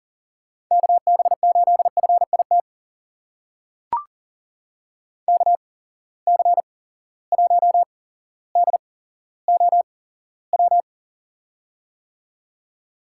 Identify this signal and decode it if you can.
Morse code, heard as a keyed tone, about 40 wpm, spaced as for 17 wpm, 700 Hz. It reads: KB8FIT KC1DOW
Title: USA Call Signs - 6 Characters - 8x Character Spacing - 40wpm